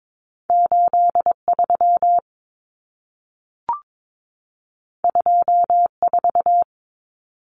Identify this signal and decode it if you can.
Morse code, heard as a keyed tone, about 22 wpm, 700 Hz. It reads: 83 24
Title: Numbers - 2-Digit 22wpm